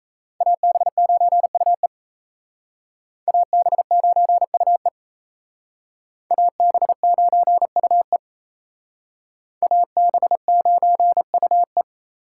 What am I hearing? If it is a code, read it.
Morse code, heard as a keyed tone, about 36 wpm, 700 Hz. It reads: AB9UE AB9UE AB9UE AB9UE